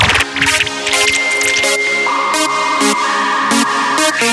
Dial-up modem handshake.